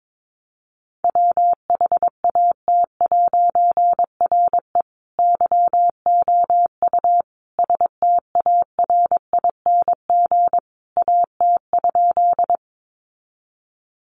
Morse code, heard as a keyed tone, about 22 wpm, 700 Hz. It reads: WHAT'RE YOU STARING AT?
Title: A Sound Of Thunder 22wpm